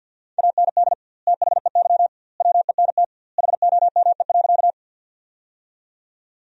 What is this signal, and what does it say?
Morse code, heard as a keyed tone, about 50 wpm, 700 Hz. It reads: AND THEY WENT HOME.
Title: 2x Word-Spacing - Sentences from Top 200 Words - 50wpm